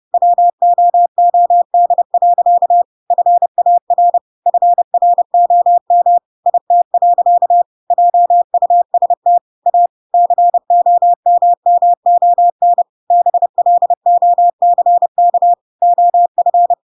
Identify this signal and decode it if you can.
Morse code, heard as a keyed tone, about 30 wpm, 700 Hz. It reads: WOOD. FAR FROM IT. JUST A COMMON BLOCK OF
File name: Pinocchio / Pinocchio ch1 30 30 ik1hge